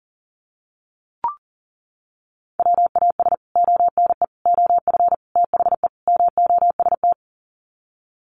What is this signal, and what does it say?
Morse code, heard as a keyed tone, about 37 wpm, 700 Hz. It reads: WAS ONE OF THE MOST
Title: Sets of 5 Words 40wpm